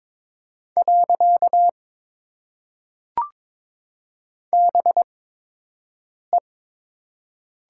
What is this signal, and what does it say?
Morse code, heard as a keyed tone, about 22 wpm, spaced as for 6 wpm, 700 Hz. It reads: . BE
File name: Top 90 QSO Elements - 8x Character Spacing - 22wpm